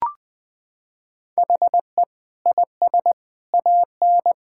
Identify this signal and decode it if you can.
Morse code, heard as a keyed tone, about 20 wpm, 700 Hz. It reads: HE IS AN
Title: Sentences with 5-Word Phrases - 20wpm